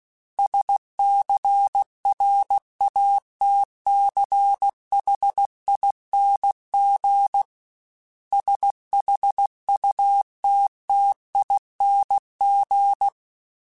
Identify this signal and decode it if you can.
Morse code, heard as a keyed tone, about 16 wpm, 800 Hz. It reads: SCRATCHING SHUTTING
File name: Special INGs / Special INGs-mp3s / Sngl ing @15